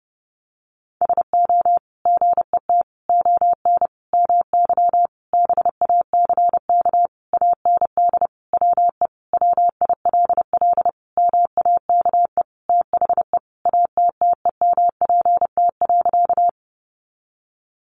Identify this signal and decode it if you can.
Morse code, heard as a keyed tone, about 30 wpm, 700 Hz. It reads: SO GET ON MY BACK AND WE WILL MAKE THE ATTEMPT.
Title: Sentences from Top 1000 Words 30wpm